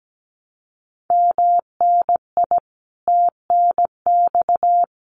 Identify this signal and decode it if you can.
Morse code, heard as a keyed tone, about 17 wpm, 700 Hz. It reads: MNI TNX
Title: Common QSO Phrases Encore 17wpm